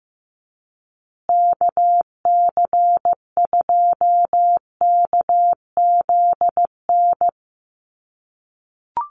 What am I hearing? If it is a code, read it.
Morse code, heard as a keyed tone, about 15 wpm, 700 Hz. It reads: KC2KZN
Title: USA Call Signs Repeated 3x 15wpm